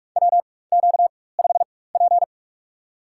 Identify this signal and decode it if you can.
Morse code, heard as a keyed tone, about 46 wpm, spaced as for 21 wpm, 700 Hz. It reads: WQ5P
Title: USA Call Signs - 4 Characters - 4x Character Spacing - 45wpm